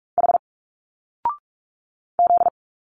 Morse code, heard as a keyed tone, about 44 wpm, 700 Hz. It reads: H Z
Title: Single Letters 45wpm